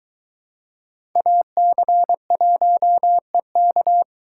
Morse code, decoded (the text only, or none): AC1EK